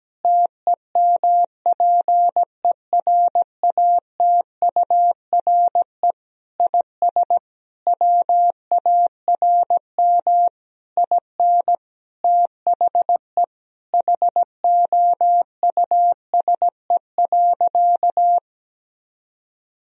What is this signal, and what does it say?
Morse code, heard as a keyed tone, about 17 wpm, 700 Hz. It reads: TEMPERATURE IS WARM IN THE HOUSE.
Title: Sentences from Top 1000 Words - Review Part 4 - 17wpm